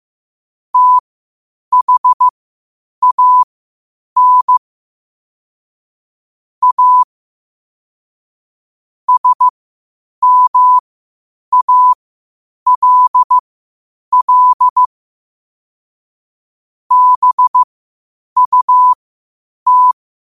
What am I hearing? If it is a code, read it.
Morse code, heard as a keyed tone, about 15 wpm, spaced as for 8 wpm, 1000 Hz. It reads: THAN A SMALL BUT
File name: qotdfiles / 8wpm